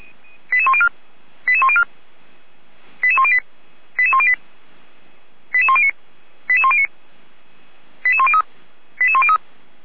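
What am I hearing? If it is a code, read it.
Five-tone selective call (ZVEI), heard as quick runs of five tones, tones of about 70 ms. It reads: 88104 88104 88108 88108 88109 88109 88103 88103
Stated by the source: ZVEI five-tone selective call received by SDR, from the Feuerwehr Orlamünde audio server.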